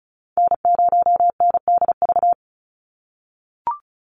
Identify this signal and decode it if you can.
Morse code, heard as a keyed tone, about 34 wpm, 700 Hz. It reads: N0NDV
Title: USA Call Signs 35wpm